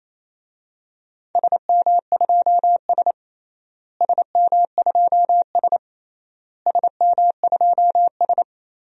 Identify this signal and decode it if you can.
Morse code, heard as a keyed tone, about 28 wpm, 700 Hz. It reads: SM2S SM2S SM2S
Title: DXCC-100 Call Signs - 4 Characters - Repeated 3x - 28wpm